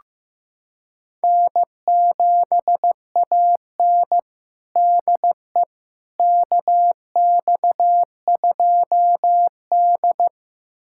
Morse code, heard as a keyed tone, about 15 wpm, 700 Hz. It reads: N7AN DE KX2D
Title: USA Callsign DE Callsign - 4 Characters - 15wpm